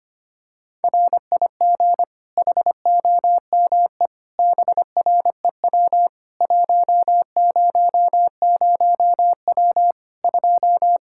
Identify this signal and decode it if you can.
Morse code, heard as a keyed tone, about 25 wpm, 700 Hz. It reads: RIG HOME BREW 100W 2
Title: Practice QSO - Exchange 2 - Info - 25wpm